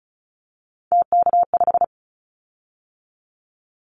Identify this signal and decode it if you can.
Morse code, heard as a keyed tone, about 34 wpm, 700 Hz. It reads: TK5
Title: Callsign Prefixes for Top 100 DXCC - 35wpm